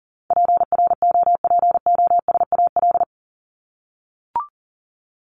Morse code, heard as a keyed tone, about 40 wpm, 700 Hz. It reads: PROPOSAL
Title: Top 1751-2000 Words 40wpm